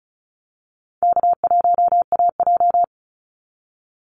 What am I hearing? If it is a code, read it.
Morse code, heard as a keyed tone, about 34 wpm, 700 Hz. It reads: K1AJ
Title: Easy Call Signs Repeated 3x - 35wpm